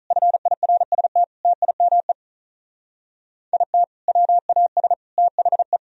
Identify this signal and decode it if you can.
Morse code, heard as a keyed tone, about 45 wpm, 700 Hz. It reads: FIRST TIME IT WAS THE